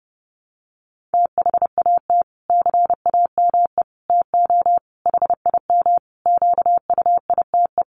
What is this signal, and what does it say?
Morse code, heard as a keyed tone, about 28 wpm, 700 Hz. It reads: THAT CAME TO HIM QUITE
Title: Sentences from Top 1300 Words 30wpm